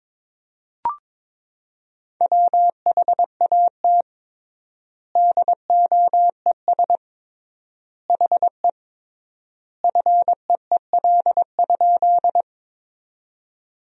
Morse code, heard as a keyed tone, about 21 wpm, 700 Hz. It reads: WHAT DOES HE FEEL?
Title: 3x Word-Spacing - Sentences from Top 400 Words - 22wpm